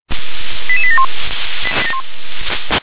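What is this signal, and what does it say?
Five-tone selective call (ZVEI), heard as quick runs of five tones, tones of about 70 ms. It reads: 88871
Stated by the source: ZVEI five-tone selective call received on SDR.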